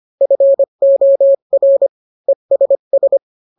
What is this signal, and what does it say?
Morse code, heard as a keyed tone, about 25 wpm, 550 Hz. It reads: FOR ESS